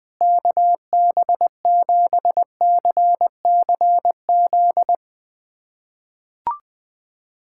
Morse code, heard as a keyed tone, about 21 wpm, 700 Hz. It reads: KB7CCZ
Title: USA Callsign DE Callsign - 6 Characters - 20wpm